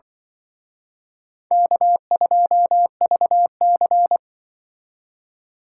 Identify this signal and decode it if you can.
Morse code, heard as a keyed tone, about 24 wpm, 700 Hz. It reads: K2VC
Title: Speed Racing - USA 1x2 Call Signs 20wpm